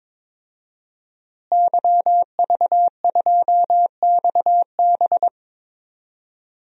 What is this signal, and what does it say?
Morse code, heard as a keyed tone, about 21 wpm, 700 Hz. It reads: YV2XB
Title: DXCC-100 Call Signs - 5 Characters - Repeated 2x - 22wpm